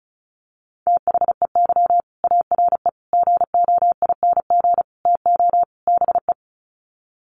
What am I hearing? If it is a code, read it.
Morse code, heard as a keyed tone, about 34 wpm, 700 Hz. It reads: THEY ARE GOING TO BE
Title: Sets of 5 Words Encore2 35wpm